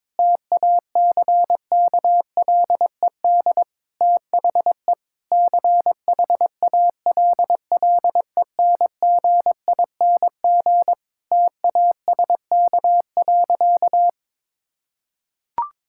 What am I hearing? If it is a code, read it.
Morse code, heard as a keyed tone, about 22 wpm, 700 Hz. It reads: TACKLED THE CHALLENGING TASK.
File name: Difficult Sentences with Hints - 22wpm